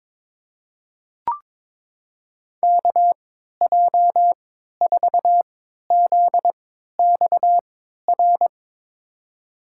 Morse code, heard as a keyed tone, about 22 wpm, spaced as for 12 wpm, 700 Hz. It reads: KJ4ZXR
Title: USA Call Signs - 6 Characters - 3x Character Spacing - 22wpm